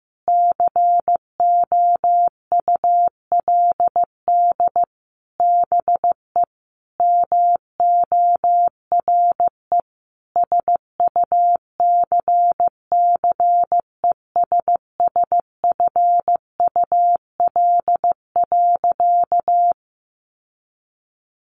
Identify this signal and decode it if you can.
Morse code, heard as a keyed tone, about 15 wpm, 700 Hz. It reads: COULD BE MORE SUCCESSFUL.
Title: Sentences from Top 900 Words 15wpm